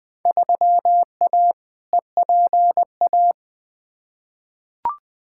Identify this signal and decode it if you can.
Morse code, heard as a keyed tone, about 20 wpm, 700 Hz. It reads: 3A EPA
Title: Field Day Exchanges - 20wpm